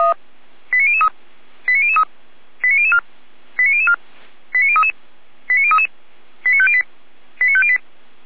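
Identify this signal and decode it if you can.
Five-tone selective call (ZVEI), heard as quick runs of five tones, tones of about 80 ms. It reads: 89003 89003 89004 89004 89033 89033 89508 89508
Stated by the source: ZVEI five-tone call 89010 received by SDR, from feuerwehr-orlamuende fire brigade server.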